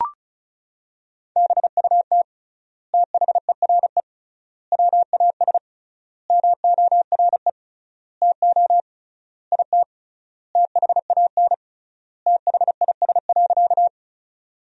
Morse code, heard as a keyed tone, about 34 wpm, 700 Hz. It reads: BUT THERE WAS MORE TO IT THAN THIS.
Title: 3x Word-Spacing - Sentences from Top 100 Words - 35wpm